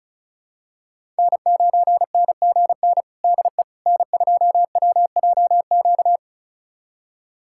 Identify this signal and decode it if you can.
Morse code, heard as a keyed tone, about 36 wpm, 700 Hz. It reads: N9NGN DE N2WJQ
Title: USA Callsign DE Callsign 35wpm